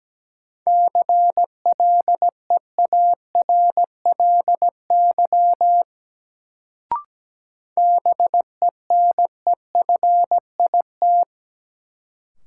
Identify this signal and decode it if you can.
Morse code, heard as a keyed tone, about 17 wpm, 700 Hz. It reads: CLEARLY BENEFIT
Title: Sentences from Top 1300 Words - New Word Drill Alt - 17wpm